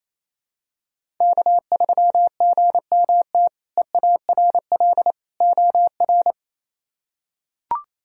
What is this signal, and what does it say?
Morse code, heard as a keyed tone, about 27 wpm, 700 Hz. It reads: K3GMT EARL OR
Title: K1USN SST QSO Practice - 28wpm